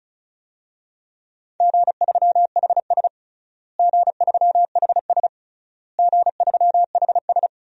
Morse code, heard as a keyed tone, about 36 wpm, 700 Hz. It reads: G3HS G3HS G3HS
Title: DXCC-100 Call Signs - 4 Characters - Repeated 3x - 35wpm